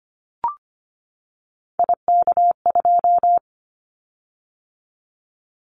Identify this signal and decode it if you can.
Morse code, heard as a keyed tone, about 25 wpm, 700 Hz. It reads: IK2
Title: Callsign Prefixes for Top 25 DXCC - 25wpm